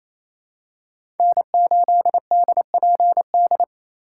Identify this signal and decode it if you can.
Morse code, heard as a keyed tone, about 28 wpm, 700 Hz. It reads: N8DPD
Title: K1USN SST QSO Practice - 28wpm